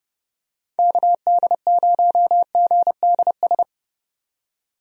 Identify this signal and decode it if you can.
Morse code, heard as a keyed tone, about 30 wpm, 700 Hz. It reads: KD0GDS
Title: USA Call Signs - 6 Characters - 30wpm